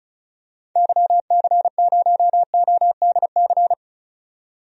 Morse code, heard as a keyed tone, about 34 wpm, 700 Hz. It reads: YC0ODC
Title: DXCC-100 Call Signs - 6 Characters - 35wpm